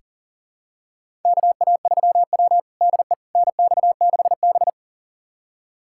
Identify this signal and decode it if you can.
Morse code, heard as a keyed tone, about 37 wpm, 700 Hz. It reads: KA3W DE NX6B
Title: USA Callsign DE Callsign - 4 Characters - 40wpm